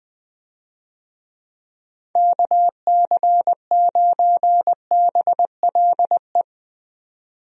Morse code, heard as a keyed tone, about 20 wpm, 700 Hz. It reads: KC9BLE
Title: USA Call Signs - 6 Characters - 20wpm